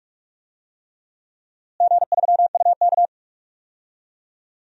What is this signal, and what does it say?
Morse code, heard as a keyed tone, about 46 wpm, 700 Hz. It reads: G3UK
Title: DXCC-100 Call Signs - 4 Characters - Repeated 3x - 45wpm